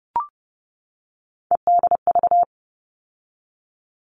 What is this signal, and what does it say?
Morse code, heard as a keyed tone, about 30 wpm, 700 Hz. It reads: EDV
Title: All Cut Numbers - 3-Digit 30wpm